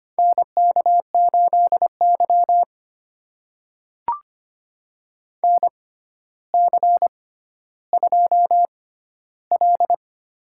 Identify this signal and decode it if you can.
Morse code, heard as a keyed tone, about 25 wpm, spaced as for 13 wpm, 700 Hz. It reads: NK8Y NC2L
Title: USA Call Signs - 4 Characters - 6x Character Spacing - 25wpm